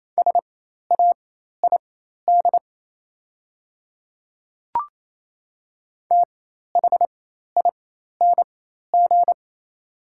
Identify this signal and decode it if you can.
Morse code, heard as a keyed tone, about 27 wpm, spaced as for 13 wpm, 690 Hz. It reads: SAID THING
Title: 4x Character Spacing - Top 100 Words - 28wpm